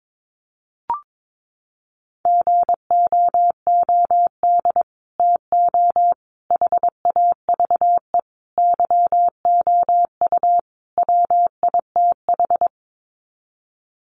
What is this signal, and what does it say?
Morse code, heard as a keyed tone, about 22 wpm, 700 Hz. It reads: GOOD TO HAVE YOU WITH
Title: Sets of 5 Words Encore2 22wpm